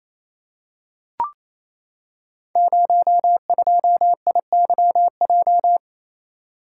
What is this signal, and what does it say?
Morse code, heard as a keyed tone, about 27 wpm, 700 Hz. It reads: 02IYJ
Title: Code Groups - 5 Letter:Numbers - 28wpm